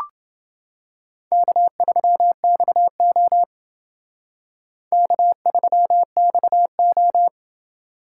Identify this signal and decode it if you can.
Morse code, heard as a keyed tone, about 28 wpm, 700 Hz. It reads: K3XO K3XO